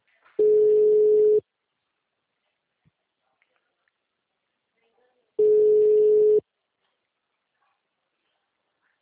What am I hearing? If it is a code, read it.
Ringback tone (Europe, 425 Hz).